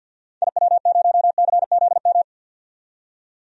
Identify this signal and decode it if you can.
Morse code, heard as a keyed tone, about 50 wpm, 700 Hz. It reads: IW0CZM
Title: DXCC-100 Call Signs - 6 Characters - 50wpm